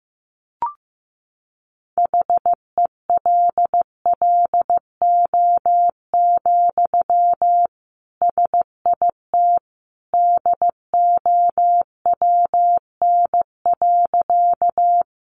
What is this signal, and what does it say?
Morse code, heard as a keyed tone, about 15 wpm, 700 Hz. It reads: HELLO, SIT DOWN.
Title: Sentences from Top 1100 Words 15wpm